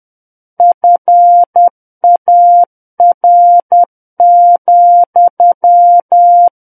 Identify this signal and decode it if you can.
Morse code, heard as a keyed tone, about 10 wpm, 700 Hz. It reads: FAR,
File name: Book One / 01-Chapter 1 - 10 WPM